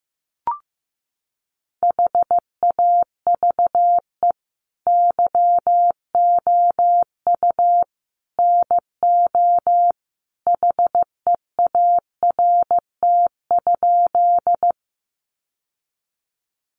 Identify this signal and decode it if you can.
Morse code, heard as a keyed tone, about 15 wpm, 700 Hz. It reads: HAVE YOU NO HEART?
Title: Sentences from Top 1200 Words 15wpm